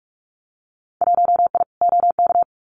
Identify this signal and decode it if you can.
Morse code, heard as a keyed tone, about 46 wpm, 700 Hz. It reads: 1I OK
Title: Winter Field Day Exchanges Encore 45wpm